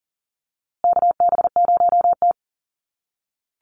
Morse code, heard as a keyed tone, about 40 wpm, 700 Hz. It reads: KB0T
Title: USA Call Signs Repeated 3x 40wpm